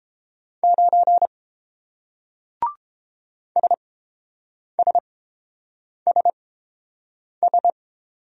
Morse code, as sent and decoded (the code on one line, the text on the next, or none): ----. / ... ... ... ...
9 SSSS